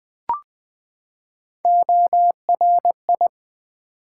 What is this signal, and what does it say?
Morse code, heard as a keyed tone, about 20 wpm, 700 Hz. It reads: ORI
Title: Build A Word - 3 Letters At A Time - 8-Letter Words - 20wpm